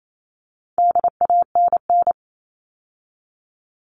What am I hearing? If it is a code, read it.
Morse code, heard as a keyed tone, about 28 wpm, 700 Hz. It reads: DANN
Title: All Cut Numbers - 4-Digit 28wpm